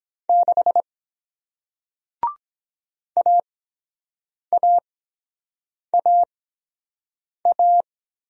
Morse code, heard as a keyed tone, about 23 wpm, spaced as for 7 wpm, 700 Hz. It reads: 6 AAAA